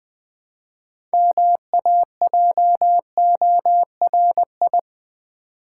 Morse code, heard as a keyed tone, about 20 wpm, 700 Hz. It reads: MAJORI
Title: Build A Word - 2 Letters At A Time - 8-Letter Words - 20wpm